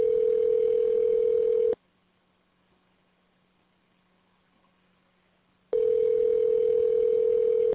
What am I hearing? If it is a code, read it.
Ringback tone (North America), 440 + 480 Hz.